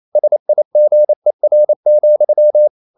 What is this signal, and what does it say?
Morse code, heard as a keyed tone, about 28 wpm, 600 Hz. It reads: SIGER,
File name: Ferie QRQ 28wpm